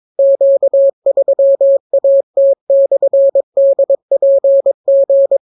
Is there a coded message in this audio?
Yes: Morse code from 0.2 to 5.4 s, heard as a keyed tone, about 22 wpm, 550 Hz. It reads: Q3AT/DPG